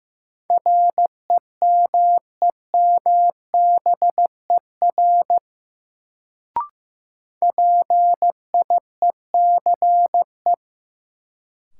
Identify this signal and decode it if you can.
Morse code, heard as a keyed tone, about 15 wpm, 700 Hz. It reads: REMEMBER PIECE
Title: Sentences from Top 400 Words - New Word Drill - 15wpm